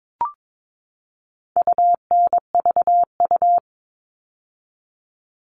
Morse code, heard as a keyed tone, about 22 wpm, 700 Hz. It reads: UNVU
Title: All Cut Numbers - 4-Digit 22wpm